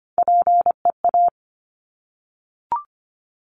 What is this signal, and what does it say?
Morse code, heard as a keyed tone, about 24 wpm, 700 Hz. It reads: PEA
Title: Top Three Letter Combinations in Words 25wpm